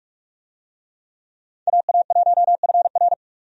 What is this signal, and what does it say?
Morse code, heard as a keyed tone, about 41 wpm, 700 Hz. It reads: AA1FR
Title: USA Call Signs - 5 Characters - 45wpm